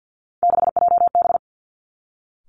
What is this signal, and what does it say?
Morse code, heard as a keyed tone, about 50 wpm, 700 Hz. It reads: 6JB
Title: Three Challenging Letter-Number Combinations 50wpm